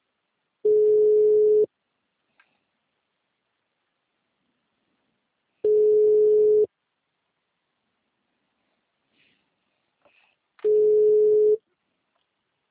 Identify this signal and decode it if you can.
Ringback tone (Europe, 425 Hz).